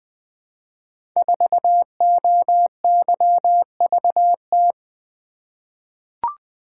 Morse code, heard as a keyed tone, about 20 wpm, 700 Hz. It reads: 4OYVT
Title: Code Groups - 5 Letter:Numbers - 20wpm